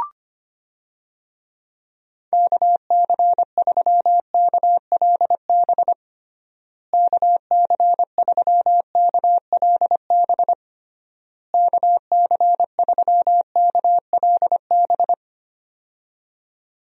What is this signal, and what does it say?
Morse code, heard as a keyed tone, about 25 wpm, 700 Hz. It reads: KC3KLB KC3KLB KC3KLB
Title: USA Call Signs - 6 Characters - Repeated 3x - 25wpm